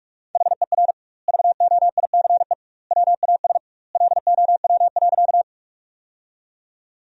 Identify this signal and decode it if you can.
Morse code, heard as a keyed tone, about 42 wpm, 700 Hz. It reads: HER VOICE WAS LOW.
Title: 2x Word-Spacing - Sentences from Top 500 Words - 45wpm